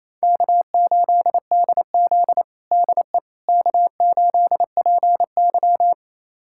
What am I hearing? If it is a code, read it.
Morse code, heard as a keyed tone, about 28 wpm, 700 Hz. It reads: K8DZ DE K8PY
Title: USA Callsign DE Callsign - 4 Characters - 28wpm